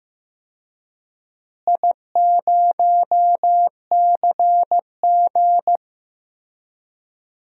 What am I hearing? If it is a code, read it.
Morse code, heard as a keyed tone, about 15 wpm, 700 Hz. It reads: I0CG
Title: DXCC-100 Call Signs - 4 Characters - Repeated 2x - 15wpm